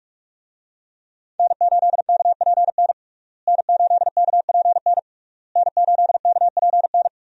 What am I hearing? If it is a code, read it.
Morse code, heard as a keyed tone, about 46 wpm, 700 Hz. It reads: N8KPN N8KPN N8KPN
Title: USA Call Signs - 5 Characters - Repeated 3x - 45wpm